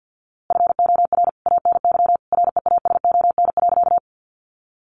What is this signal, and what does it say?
Morse code, heard as a keyed tone, about 49 wpm, 700 Hz. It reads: FOR ANY REASON.
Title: A Sound Of Thunder 50wpm